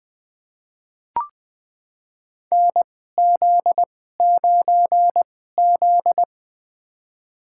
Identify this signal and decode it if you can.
Morse code, heard as a keyed tone, about 21 wpm, spaced as for 15 wpm, 700 Hz. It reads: NZ9Z
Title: USA Call Signs - 4 Characters - 2x Character Spacing - 20wpm